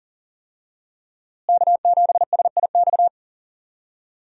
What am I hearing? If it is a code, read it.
Morse code, heard as a keyed tone, about 40 wpm, 700 Hz. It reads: K7SIX
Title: USA Call Signs - 5 Characters - Repeated 2x - 40wpm